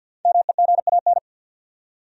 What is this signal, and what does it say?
Morse code, heard as a keyed tone, about 50 wpm, 700 Hz. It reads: MEGAN
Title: USA Common Names - Max 5 Letters - 50wpm